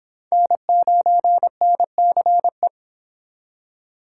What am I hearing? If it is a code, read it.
Morse code, heard as a keyed tone, about 25 wpm, 700 Hz. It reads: N9NCE